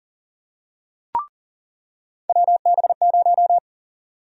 Morse code, heard as a keyed tone, about 40 wpm, 700 Hz. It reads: WB0
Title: USA Call Sign Prefixes - 40wpm